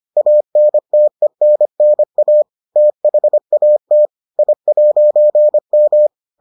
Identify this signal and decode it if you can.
Morse code, heard as a keyed tone, about 25 wpm, 600 Hz. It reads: ANTENNA THAT I'M